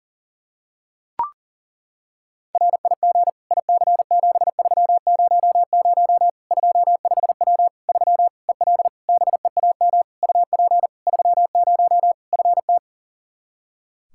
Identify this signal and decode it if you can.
Morse code, heard as a keyed tone, about 40 wpm, 700 Hz. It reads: RIG IC7300 25W 3 EL BEAM UP 30 FT
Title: Practice QSO - Exchange 2 - Info - 40wpm